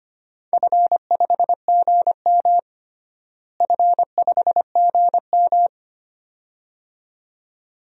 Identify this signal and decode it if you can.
Morse code, heard as a keyed tone, about 25 wpm, 700 Hz. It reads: F5GM F5GM
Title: DXCC-100 Call Signs - 4 Characters - Repeated 2x - 25wpm